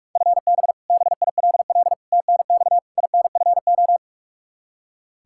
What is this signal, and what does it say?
Morse code, heard as a keyed tone, about 46 wpm, 700 Hz. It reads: FB BILL TNX INFO
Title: Practice QSO - Exchange 3 - More Info - 45wpm